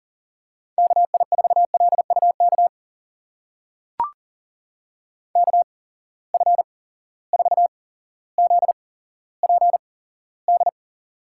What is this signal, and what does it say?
Morse code, heard as a keyed tone, about 40 wpm, spaced as for 17 wpm, 700 Hz. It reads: KI4LUK KF4ZPD
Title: USA Call Signs - 6 Characters - 8x Character Spacing - 40wpm